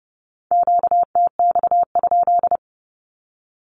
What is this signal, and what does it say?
Morse code, heard as a keyed tone, about 30 wpm, 700 Hz. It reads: QTX?
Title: Q-Codes with Spoken Meaning - 30wpm